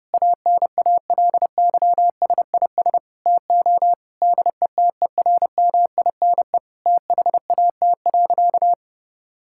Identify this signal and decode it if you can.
Morse code, heard as a keyed tone, about 32 wpm, 700 Hz. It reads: ANALYSIS TO DETERMINE THAT.
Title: Sentences from Top 1000 Words - Review Part 5 - 30wpm